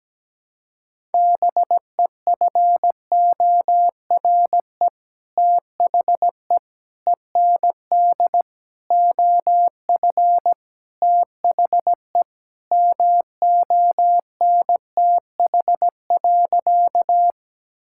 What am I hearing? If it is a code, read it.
Morse code, heard as a keyed tone, about 17 wpm, 700 Hz. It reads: BEFORE THE END OF THE MONTH.
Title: Sentences from Top 1000 Words - Review Part 5 - 17wpm